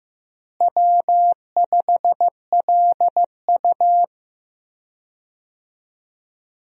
Morse code, heard as a keyed tone, about 15 wpm, 700 Hz. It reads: W5LU
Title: USA Call Signs - 4 Characters - Repeated 3x - 15wpm